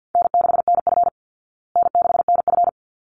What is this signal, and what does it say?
Morse code, heard as a keyed tone, about 50 wpm, 700 Hz. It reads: N6NF N6NF
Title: Easy Call Signs Repeated 2x - 50wpm